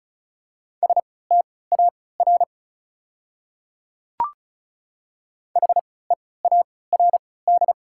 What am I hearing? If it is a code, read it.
Morse code, heard as a keyed tone, about 34 wpm, spaced as for 20 wpm, 690 Hz. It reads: STAR HEARD
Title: 3x Character Spacing - Top 500 Words - 35wpm